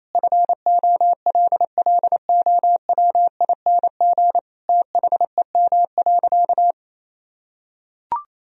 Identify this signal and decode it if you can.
Morse code, heard as a keyed tone, about 27 wpm, 700 Hz. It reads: FOLLOWING THEM.
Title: Sentences from Top 1000 Words - Review Part 4 - 28wpm